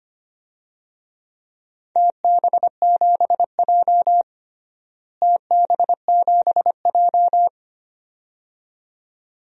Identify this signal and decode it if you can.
Morse code, heard as a keyed tone, about 25 wpm, 700 Hz. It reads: TB7J TB7J
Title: DXCC-100 Call Signs - 4 Characters - Repeated 2x - 25wpm